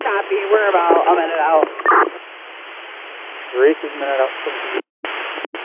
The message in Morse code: none

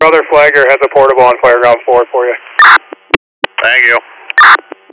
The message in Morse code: none

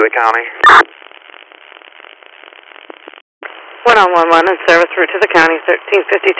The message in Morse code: none

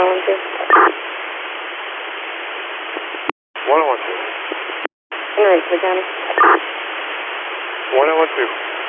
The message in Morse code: none

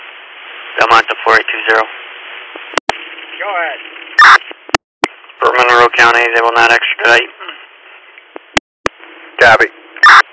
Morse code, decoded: none

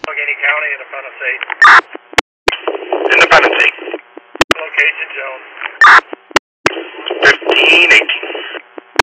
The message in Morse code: none

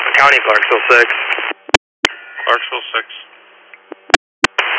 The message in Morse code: none

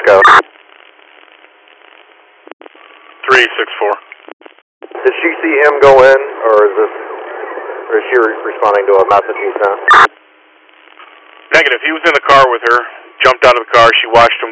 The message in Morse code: none

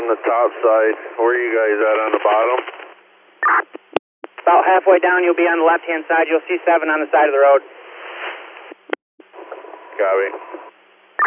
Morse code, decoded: none

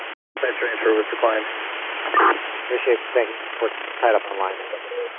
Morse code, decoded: none

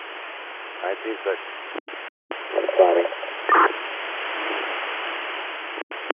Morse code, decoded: none

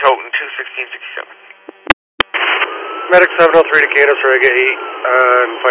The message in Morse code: none

